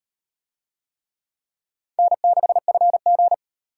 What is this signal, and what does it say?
Morse code, heard as a keyed tone, about 35 wpm, 700 Hz. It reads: N6FG